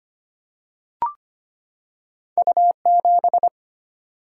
Morse code, heard as a keyed tone, about 25 wpm, 700 Hz. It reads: U 7
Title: Single-Letter Single-Number 25wpm